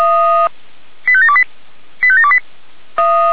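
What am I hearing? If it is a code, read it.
Five-tone selective call (ZVEI), heard as quick runs of five tones, tones of about 70 ms. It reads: 85728 85728
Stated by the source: Received on an SDR as a ZVEI five-tone call.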